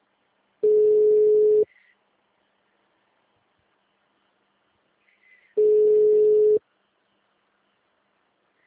Ringback tone (Europe, 425 Hz).
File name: out- -2010- - - .12